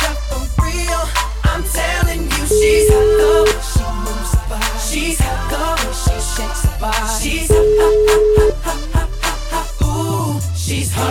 Ringback tone (Europe, 425 Hz), with noise.